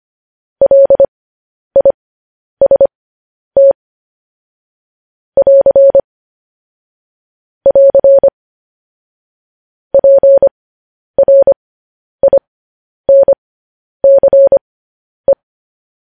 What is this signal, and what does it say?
Morse code, heard as a keyed tone, about 25 wpm, spaced as for 10 wpm, 550 Hz. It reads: LIST <AR> <AR> PRINCE
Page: news headlines in morse code at 10 wpm